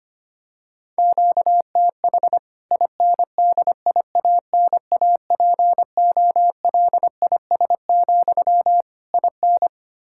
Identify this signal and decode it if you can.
Morse code, heard as a keyed tone, about 25 wpm, 700 Hz. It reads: QTH INDIANAPOLIS, IN
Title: K1USN SST QSO Practice 25wpm